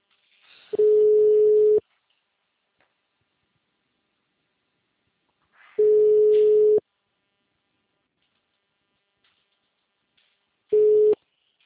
Ringback tone (Europe, 425 Hz).
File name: out- -2007- - - .68981